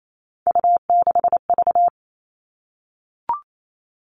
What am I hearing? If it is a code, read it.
Morse code, heard as a keyed tone, about 28 wpm, 700 Hz. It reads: U6V
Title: All Cut Numbers - 3-Digit 28wpm